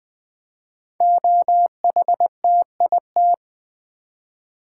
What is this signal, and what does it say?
Morse code, heard as a keyed tone, about 20 wpm, 700 Hz. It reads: OHTIT
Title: Code Groups - 5 Letters - 20wpm